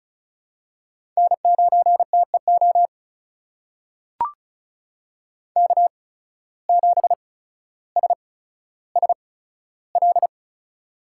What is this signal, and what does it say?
Morse code, heard as a keyed tone, about 36 wpm, spaced as for 15 wpm, 700 Hz. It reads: N9TEO K7SSL
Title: USA Call Signs - 5 Characters - 8x Character Spacing - 35wpm